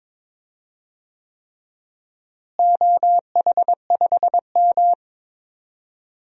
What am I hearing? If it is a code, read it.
Morse code, heard as a keyed tone, about 22 wpm, 700 Hz. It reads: OH5M
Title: DXCC-100 Call Signs - 4 Characters - Repeated 3x - 22wpm